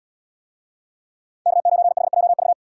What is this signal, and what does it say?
Morse code, heard as a keyed tone, about 77 wpm, 700 Hz. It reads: K9HJV